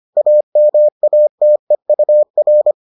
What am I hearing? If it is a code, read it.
Morse code, heard as a keyed tone, about 25 wpm, 600 Hz. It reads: AMATEUR